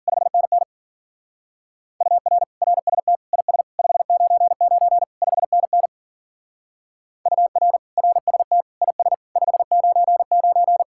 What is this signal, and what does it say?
Morse code, heard as a keyed tone, about 48 wpm, 700 Hz. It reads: 5NN UR RST IS 599 5NN UR RST IS 599
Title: Speed-Racing Common QSO Phrases - 40wpm